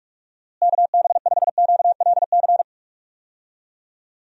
Morse code, heard as a keyed tone, about 44 wpm, 700 Hz. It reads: KB5QLC
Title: USA Call Signs - 6 Characters - Repeated 2x - 45wpm